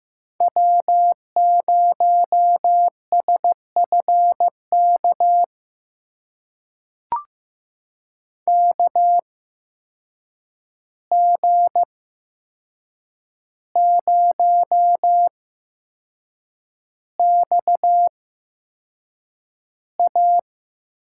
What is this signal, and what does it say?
Morse code, heard as a keyed tone, about 15 wpm, spaced as for 6 wpm, 700 Hz. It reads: W0SFK KG0XA
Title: USA Call Signs - 5 Characters - 8x Character Spacing - 15wpm